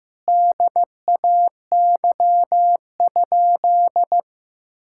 Morse code, heard as a keyed tone, about 15 wpm, 700 Hz. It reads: DAY?
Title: Sentences from Top 1000 Words - Review Part 5 - 15wpm